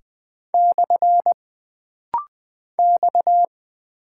Morse code, heard as a keyed tone, about 20 wpm, 700 Hz. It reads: / X
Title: Single Character (Letter:Number:Punctuation:Prosign) - Rapid-Fire - 20wpm